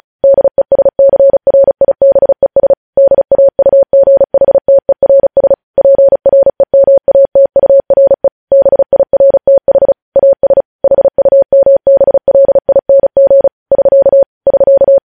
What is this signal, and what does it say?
Morse code, heard as a keyed tone, about 36 wpm, 550 Hz. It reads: DESCRIBES DAUGHTERS PREMATURE BIRTH AS HUMBLING <SK> <SK>
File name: podcast / News headlines-1x-35x35